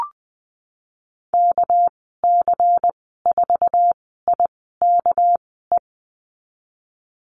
Morse code, heard as a keyed tone, about 20 wpm, spaced as for 14 wpm, 700 Hz. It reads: KC4IKE
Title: 245 Lesson 39g - Letter z - Call Signs 20wpm